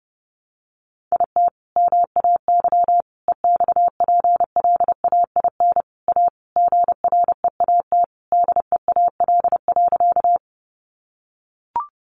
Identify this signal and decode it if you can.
Morse code, heard as a keyed tone, about 28 wpm, 700 Hz. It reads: IT MAY EXPLAIN A GREAT DEAL.
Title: Sentences from Top 1100 Words 30wpm